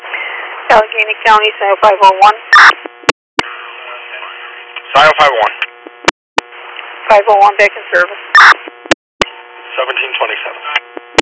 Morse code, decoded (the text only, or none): none